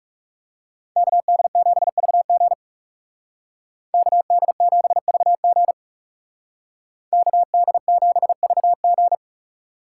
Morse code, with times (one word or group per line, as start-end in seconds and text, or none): 1.0-2.5 KD7VG
3.9-5.7 KD7VG
7.1-9.2 KD7VG